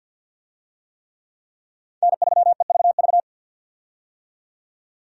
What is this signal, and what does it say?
Morse code, heard as a keyed tone, about 52 wpm, 700 Hz. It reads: N3EVV
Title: USA Call Signs - 5 Characters - 50wpm